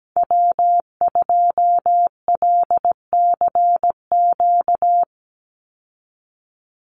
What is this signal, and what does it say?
Morse code, heard as a keyed tone, about 17 wpm, 700 Hz. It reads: W2LCQ
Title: Call Signs Repeated 3x - 17wpm